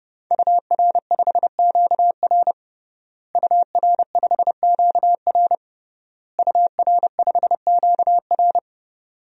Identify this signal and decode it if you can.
Morse code, heard as a keyed tone, about 30 wpm, 700 Hz. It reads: UR5QR UR5QR UR5QR
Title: DXCC-100 Call Signs - 5 Characters - Repeated 3x - 30wpm